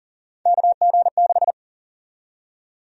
Morse code, heard as a keyed tone, about 40 wpm, 700 Hz. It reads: KG6
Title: USA Call Sign Prefixes - 40wpm